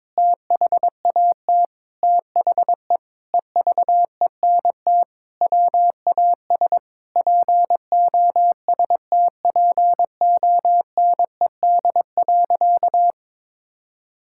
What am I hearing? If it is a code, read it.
Morse code, heard as a keyed tone, about 22 wpm, 700 Hz. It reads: THAT THE EVENT WAS POSTPONED.
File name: practice / 22wpm / Sentences with 8-Word Phrases - 22wpm